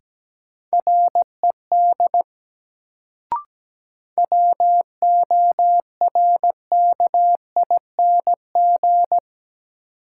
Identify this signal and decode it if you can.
Morse code, heard as a keyed tone, about 17 wpm, 700 Hz. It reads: RED WORKING
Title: Sentences from Top 700 Words - New Word Drill - 17wpm